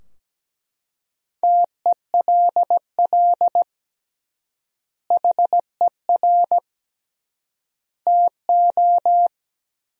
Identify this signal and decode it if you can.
Morse code, heard as a keyed tone, about 17 wpm, 700 Hz. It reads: TELL HER TO
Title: 3x Word-Spacing - Sentences from Top 300 Words - 17wpm